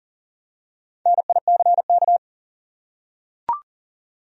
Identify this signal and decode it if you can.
Morse code, heard as a keyed tone, about 40 wpm, 700 Hz. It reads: NICK
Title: USA Common Names - Max 4 Letters - 40wpm